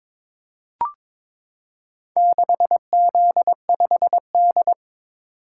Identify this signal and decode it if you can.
Morse code, heard as a keyed tone, about 22 wpm, 700 Hz. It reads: 6Z5D
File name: practice / 22wpm / Four Letter-Number Combinations - 22wpm